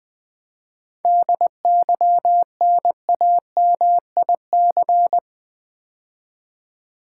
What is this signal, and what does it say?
Morse code, heard as a keyed tone, about 20 wpm, 700 Hz. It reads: DYNAMIC
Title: 7 Letter Words - First 3 Letters Given - 20wpm